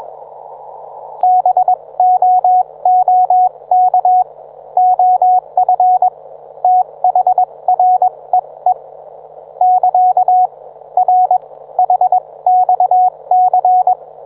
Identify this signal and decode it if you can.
Morse code, heard as a keyed tone, about 21 wpm, spaced as for 18 wpm, 730 Hz. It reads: BOOK OF THREE <KA> R HXC
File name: book